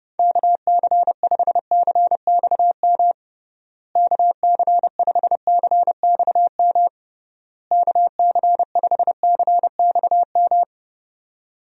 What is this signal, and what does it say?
Morse code, heard as a keyed tone, about 32 wpm, 700 Hz. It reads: KC5CXM KC5CXM KC5CXM
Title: USA Call Signs - 6 Characters - Repeated 3x - 30wpm